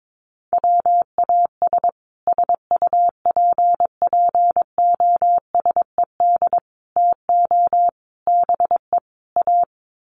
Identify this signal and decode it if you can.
Morse code, heard as a keyed tone, about 22 wpm, 700 Hz. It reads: WAS SUPPOSED TO BE A
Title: Sets of 5 Words Encore 22wpm